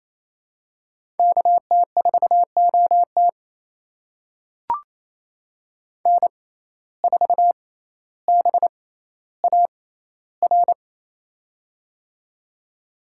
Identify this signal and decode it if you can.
Morse code, heard as a keyed tone, about 27 wpm, spaced as for 14 wpm, 700 Hz. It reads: KT4OT N4BAR
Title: USA Call Signs - 5 Characters - 6x Character Spacing - 28wpm